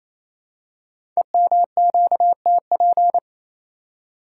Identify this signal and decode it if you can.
Morse code, heard as a keyed tone, about 27 wpm, 700 Hz. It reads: EMQTP
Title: Code Groups - 5 Letters - 28wpm